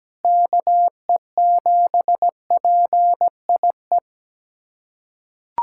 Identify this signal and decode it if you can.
Morse code, heard as a keyed tone, about 17 wpm, 700 Hz. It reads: KE7PIE
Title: USA Callsign DE Callsign - 6 Characters - 17wpm